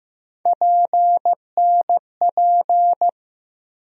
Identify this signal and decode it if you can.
Morse code, heard as a keyed tone, about 15 wpm, 700 Hz. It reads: PNP
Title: USA Call Sign Postfixes - 3 Letters 15wpm